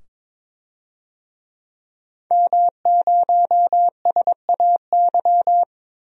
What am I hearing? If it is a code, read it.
Morse code, heard as a keyed tone, about 22 wpm, 700 Hz. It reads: M0SAY
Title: DXCC-100 Call Signs - 5 Characters - 22wpm